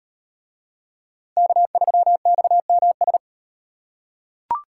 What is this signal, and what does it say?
Morse code, heard as a keyed tone, about 38 wpm, 700 Hz. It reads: K3XMS